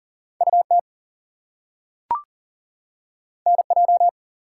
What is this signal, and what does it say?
Morse code, heard as a keyed tone, about 40 wpm, 700 Hz. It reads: UT NJ
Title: US States and Canadian Provinces 40wpm